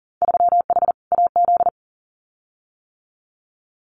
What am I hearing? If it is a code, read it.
Morse code, heard as a keyed tone, about 42 wpm, 700 Hz. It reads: 3H AZ
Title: Winter Field Day Exchanges Encore 40wpm